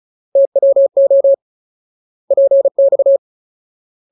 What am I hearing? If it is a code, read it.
Morse code, heard as a keyed tone, about 34 wpm, 550 Hz. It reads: TWO PX
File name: Ultimate Word List / 35WPM Ultimate